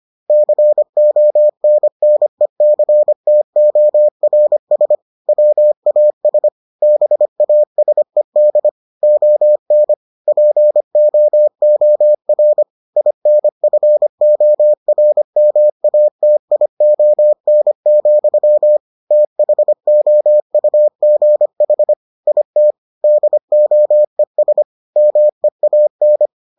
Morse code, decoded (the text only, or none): CONNECTORS WAS BASED ON POOR INFORMATION, THOUGH IT DOES MEAN